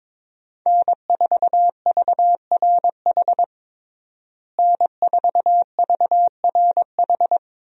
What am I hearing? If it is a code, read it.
Morse code, heard as a keyed tone, about 22 wpm, 700 Hz. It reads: N4VRH N4VRH
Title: USA Call Signs - 5 Characters - Repeated 2x - 22wpm